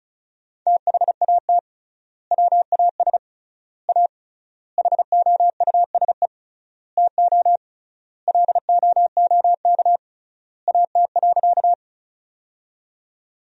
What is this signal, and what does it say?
Morse code, heard as a keyed tone, about 34 wpm, 700 Hz. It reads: THAT WAS A HOUSE TO LOOK AT.
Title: 3x Word-Spacing - Sentences from Top 200 Words - 35wpm